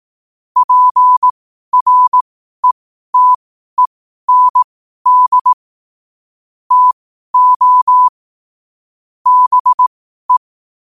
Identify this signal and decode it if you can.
Morse code, heard as a keyed tone, about 18 wpm, spaced as for 12 wpm, 1000 Hz. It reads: PRETEND TO BE
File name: qotdfiles / 12wpm